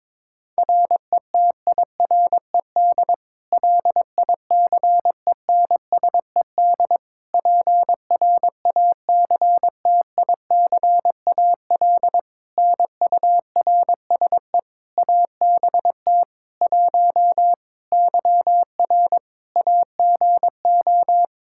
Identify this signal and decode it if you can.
Morse code, heard as a keyed tone, about 22 wpm, 700 Hz. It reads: RETIRED LICENSED PRACTICAL NURSE ABT 1 YR AGO
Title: Practice QSO - Exchange 3 - More Info - 22wpm